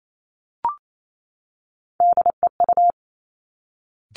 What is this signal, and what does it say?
Morse code, heard as a keyed tone, about 28 wpm, 700 Hz. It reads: DEU
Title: Three Challenging Letter-Number Combinations 28wpm